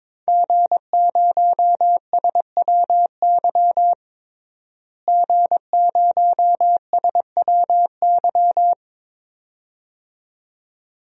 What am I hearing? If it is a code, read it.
Morse code, heard as a keyed tone, about 22 wpm, 700 Hz. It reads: G0SWY G0SWY
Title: DXCC-100 Call Signs - 5 Characters - Repeated 2x - 22wpm